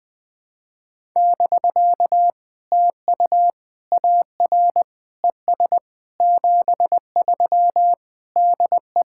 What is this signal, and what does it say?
Morse code, heard as a keyed tone, about 20 wpm, 700 Hz. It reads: <BK> TU AR ES 73 DE
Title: POTA QSO Practice - 20wpm